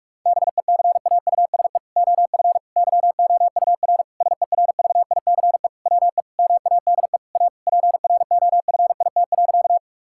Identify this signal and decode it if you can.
Morse code, heard as a keyed tone, about 46 wpm, 700 Hz. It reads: BECAUSE OF YOUR SERVICE WE MADE A PROFIT.
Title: Sentences from Top 1000 Words - Review Part 5 - 45wpm